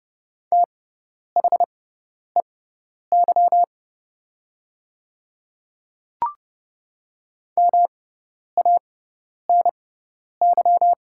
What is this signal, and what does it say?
Morse code, heard as a keyed tone, about 30 wpm, spaced as for 10 wpm, 700 Hz. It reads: THEY MANY
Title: 6x Character Spacing - Top 100 Words - 30wpm